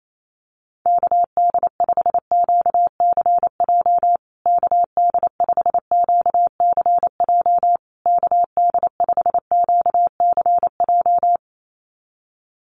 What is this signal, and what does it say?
Morse code, heard as a keyed tone, about 28 wpm, 700 Hz. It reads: KD5QCJ KD5QCJ KD5QCJ
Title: USA Call Signs Repeated 3x 28wpm